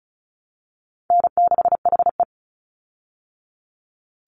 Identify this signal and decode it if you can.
Morse code, heard as a keyed tone, about 34 wpm, 700 Hz. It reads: N6HE
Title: Easy Call Signs Repeated 3x - 35wpm